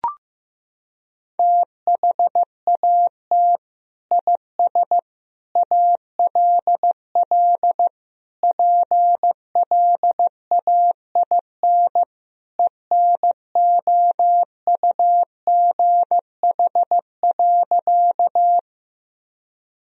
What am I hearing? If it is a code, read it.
Morse code, heard as a keyed tone, about 15 wpm, 700 Hz. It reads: THAT IS ALL PLAIN ENOUGH.
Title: Sentences from Top 1000 Words - Review Part 2 - 15wpm